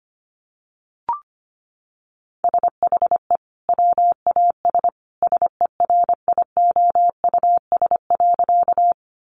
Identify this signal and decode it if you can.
Morse code, heard as a keyed tone, about 25 wpm, 690 Hz. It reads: SHE WAS SERIOUS.
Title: Sentences from Top 1100 Words 25wpm